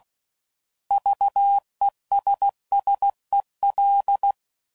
Morse code, heard as a keyed tone, about 16 wpm, 800 Hz. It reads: VESSEL
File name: Prefixes / By First Letter / Vs